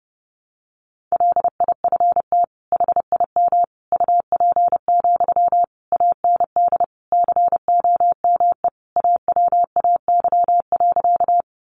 Morse code, heard as a keyed tone, about 30 wpm, 700 Hz. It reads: LIFT HIM UP, AND COME AWAY.
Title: Sentences from Top 1000 Words 30wpm